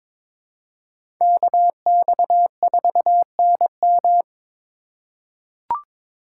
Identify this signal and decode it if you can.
Morse code, heard as a keyed tone, about 22 wpm, 700 Hz. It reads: KX4NM
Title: K1USN SST QSO Practice - 22wpm